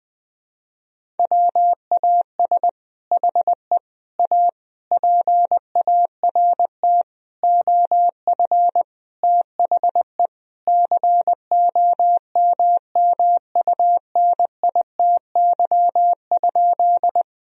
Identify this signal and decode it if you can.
Morse code, heard as a keyed tone, about 20 wpm, 700 Hz. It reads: WAS HE A PART OF THE COMMUNITY?
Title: Sentences from Top 1000 Words - Review Part 4 - 20wpm